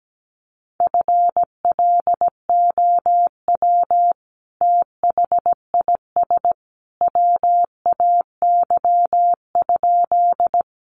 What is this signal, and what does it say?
Morse code, heard as a keyed tone, about 17 wpm, 700 Hz. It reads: FLOW THIS WAY?
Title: Sentences from Top 1200 Words 17wpm